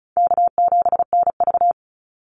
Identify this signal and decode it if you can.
Morse code, heard as a keyed tone, about 34 wpm, 700 Hz. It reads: K7NV
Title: Easy Call Signs Repeated 2x - 35wpm